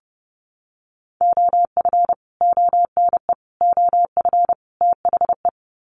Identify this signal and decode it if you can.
Morse code, heard as a keyed tone, about 30 wpm, 700 Hz. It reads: OF ONE OF THE
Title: Sets of 4 Words 30wpm